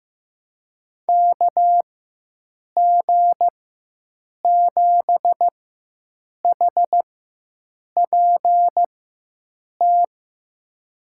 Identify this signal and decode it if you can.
Morse code, heard as a keyed tone, about 15 wpm, spaced as for 7 wpm, 700 Hz. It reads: KG7HPT
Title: USA Call Signs - 6 Characters - 4x Character Spacing - 15wpm